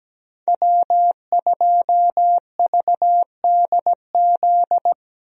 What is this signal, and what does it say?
Morse code, heard as a keyed tone, about 17 wpm, 700 Hz. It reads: W2VDZ
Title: USA Call Signs - 5 Characters - Repeated 2x - 17wpm